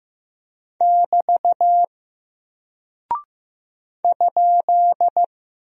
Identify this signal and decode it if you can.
Morse code, heard as a keyed tone, about 15 wpm, 700 Hz. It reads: <BT> ?
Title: Single Character (Letter:Number:Punctuation:Prosign) Alt 15wpm